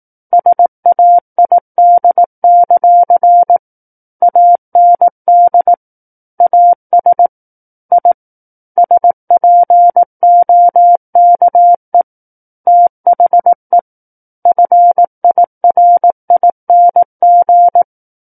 Morse code, heard as a keyed tone, about 18 wpm, 700 Hz. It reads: SAID; AND AS I SPOKE THE FIRING
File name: Book One / 09-Chapter 9 - 18 WPM